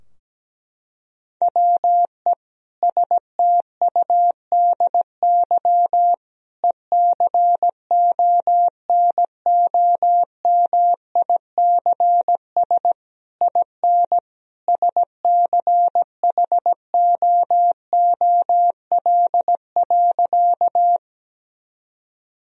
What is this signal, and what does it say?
Morse code, heard as a keyed tone, about 17 wpm, 700 Hz. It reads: WE STUDY ECONOMICS IN SCHOOL.
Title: Sentences from Top 1000 Words - Review Part 3 - 17wpm